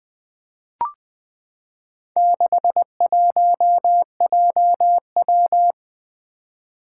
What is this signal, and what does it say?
Morse code, heard as a keyed tone, about 20 wpm, 700 Hz. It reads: 61JW
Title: Four Challenging Letter-Number Combinations 20wpm